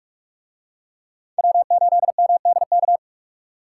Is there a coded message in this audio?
Yes: Morse code from 1.4 to 3.0 s, heard as a keyed tone, about 44 wpm, 700 Hz. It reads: W8MDK